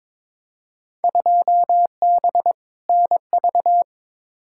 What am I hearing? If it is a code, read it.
Morse code, heard as a keyed tone, about 22 wpm, 700 Hz. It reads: 2B NV
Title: Field Day Exchanges Encore - 22wpm